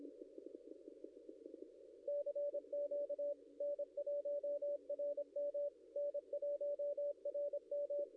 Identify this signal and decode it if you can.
Morse code, heard as a keyed tone, about 25 wpm, 560 Hz. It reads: CQ N1RM N1RM
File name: contests / ss-cw-k5tr-op-n5rz -0700